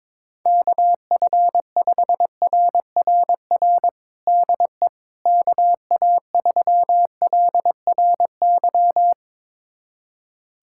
Morse code, heard as a keyed tone, about 21 wpm, 700 Hz. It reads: KF5RRR DE KA3LRY
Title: USA Callsign DE Callsign - 6 Characters - 22wpm